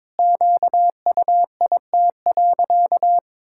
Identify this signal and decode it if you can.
Morse code, heard as a keyed tone, about 22 wpm, 700 Hz. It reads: QUIT.
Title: Pangram Challenge v2 - 22wpm